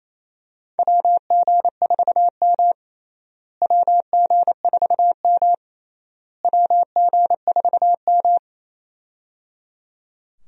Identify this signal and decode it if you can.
Morse code, heard as a keyed tone, about 27 wpm, 700 Hz. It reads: WG4M WG4M WG4M
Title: USA Call Signs - 4 Characters - Repeated 3x - 28wpm